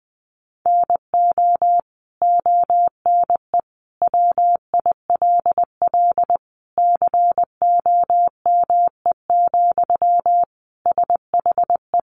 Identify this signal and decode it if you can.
Morse code, heard as a keyed tone, about 20 wpm, 700 Hz. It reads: NO ONE WILL COME, SHE
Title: Sentences from Top 100 Words 20wpm